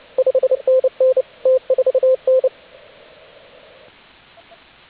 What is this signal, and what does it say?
Morse code, heard as a keyed tone, about 29 wpm, 520 Hz, with radio noise. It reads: 5NN T4N
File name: AUDIO2019 / HG8DX CW